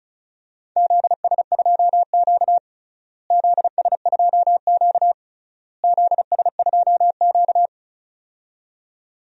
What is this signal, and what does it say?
Morse code, heard as a keyed tone, about 36 wpm, 700 Hz. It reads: ZS2Q ZS2Q ZS2Q
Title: DXCC-100 Call Signs - 4 Characters - Repeated 3x - 35wpm